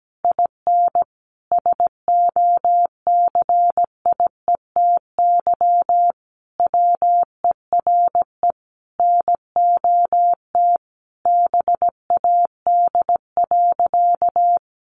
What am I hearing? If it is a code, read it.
Morse code, heard as a keyed tone, about 17 wpm, 700 Hz. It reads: IN SOCIETY WERE NOT BAD.
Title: Sentences from Top 900 Words 17wpm